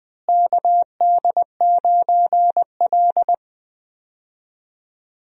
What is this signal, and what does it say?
Morse code, heard as a keyed tone, about 21 wpm, 700 Hz. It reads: KD9L
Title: USA Call Signs - 4 Characters - Repeated 3x - 20wpm